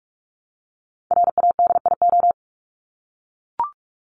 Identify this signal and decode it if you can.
Morse code, heard as a keyed tone, about 44 wpm, 700 Hz. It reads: RADIO
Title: Top 501-750 Words 45wpm